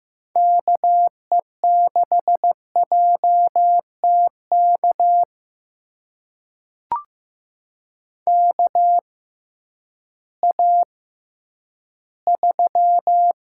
Morse code, heard as a keyed tone, about 15 wpm, spaced as for 10 wpm, 700 Hz. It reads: KE6JTK KA3
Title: USA Call Signs - 6 Characters - 6x Character Spacing - 15wpm